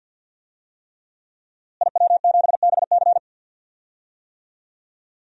Morse code, heard as a keyed tone, about 52 wpm, 700 Hz. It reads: IW7BC
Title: DXCC-100 Call Signs - 5 Characters - Repeated 3x - 50wpm